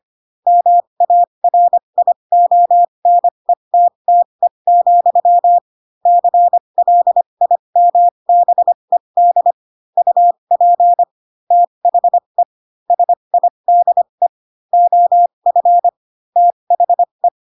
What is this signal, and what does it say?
Morse code, heard as a keyed tone, about 25 wpm, spaced as for 22 wpm, 700 Hz. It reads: MARIONETTE, CLIMBED UP THE SIDE OF THE
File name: Pinocchio / Pinocchio ch7 18 25 ik1hge